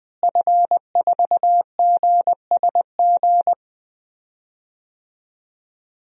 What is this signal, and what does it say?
Morse code, heard as a keyed tone, about 20 wpm, 700 Hz. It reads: F4GSG
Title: DXCC-100 Call Signs - 5 Characters - Repeated 3x - 20wpm